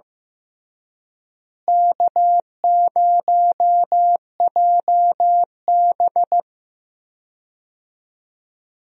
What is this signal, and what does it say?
Morse code, heard as a keyed tone, about 15 wpm, 700 Hz. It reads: K0JB
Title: USA Call Signs - 4 Characters - Repeated 3x - 15wpm